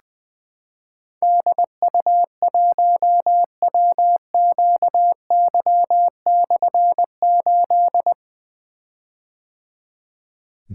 Morse code, heard as a keyed tone, about 20 wpm, 700 Hz. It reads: DU1WQY/8
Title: Difficult Call Signs 20wpm